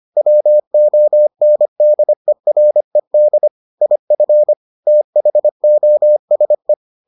Morse code, heard as a keyed tone, about 25 wpm, 600 Hz. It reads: WONDERED IF THOSE